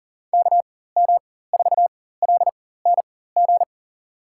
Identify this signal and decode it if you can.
Morse code, heard as a keyed tone, about 40 wpm, spaced as for 19 wpm, 700 Hz. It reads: KM4LNG
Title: USA Call Signs - 6 Characters - 4x Character Spacing - 40wpm